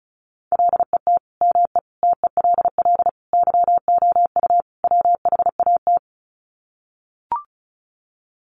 Morse code, heard as a keyed tone, about 34 wpm, 700 Hz. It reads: LET ME TELL YOU WHAT
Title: Sets of 5 Words Encore 35wpm